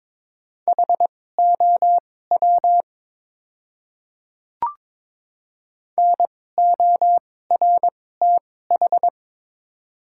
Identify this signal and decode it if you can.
Morse code, heard as a keyed tone, about 21 wpm, spaced as for 16 wpm, 700 Hz. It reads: HOW NORTH
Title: 2x Character Spacing - Top 300 Words - 22wpm